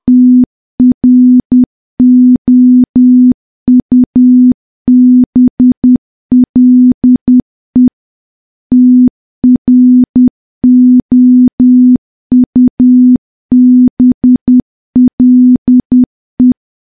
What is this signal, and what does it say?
Morse code, heard as a keyed tone, about 10 wpm, 250 Hz. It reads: TROUBLE TROUBLE
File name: morse / morse-training-10WPM